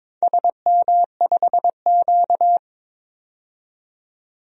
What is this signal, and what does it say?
Morse code, heard as a keyed tone, about 22 wpm, 700 Hz. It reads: SM5Q
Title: DXCC-100 Call Signs - 4 Characters - 22wpm